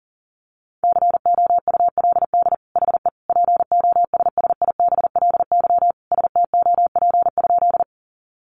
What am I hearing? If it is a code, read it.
Morse code, heard as a keyed tone, about 40 wpm, 700 Hz. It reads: COULD HE POSSIBLY STOP?
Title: Sentences from Top 1300 Words 40wpm